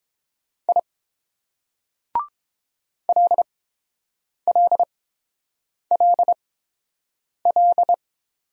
Morse code, heard as a keyed tone, about 28 wpm, spaced as for 7 wpm, 700 Hz. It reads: I LLLL